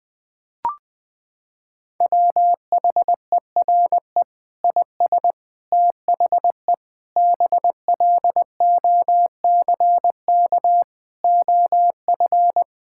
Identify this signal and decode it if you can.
Morse code, heard as a keyed tone, about 20 wpm, 700 Hz. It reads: WHERE IS THE BLOCK OF
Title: Sentences from Top 1600 Words 20wpm